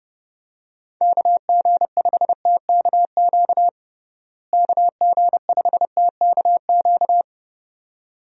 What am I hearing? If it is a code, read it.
Morse code, heard as a keyed tone, about 30 wpm, 700 Hz. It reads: KG5TKQ KG5TKQ
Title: USA Call Signs - 6 Characters - Repeated 2x - 30wpm